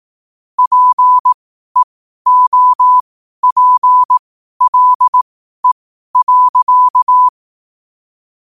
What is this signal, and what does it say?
Morse code, heard as a keyed tone, about 18 wpm, spaced as for 12 wpm, 1000 Hz. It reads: PEOPLE.
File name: qotdfiles / 12wpm